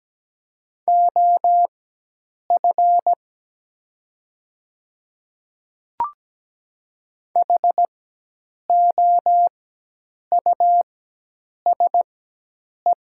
Morse code, heard as a keyed tone, about 17 wpm, spaced as for 8 wpm, 700 Hz. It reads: OF HOUSE
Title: 4x Character Spacing - Top 200 Words - 17wpm